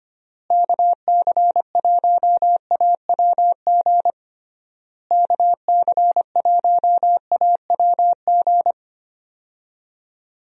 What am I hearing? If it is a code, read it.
Morse code, heard as a keyed tone, about 25 wpm, 700 Hz. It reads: KC1AWG KC1AWG
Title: USA Call Signs - 6 Characters - Repeated 3x - 25wpm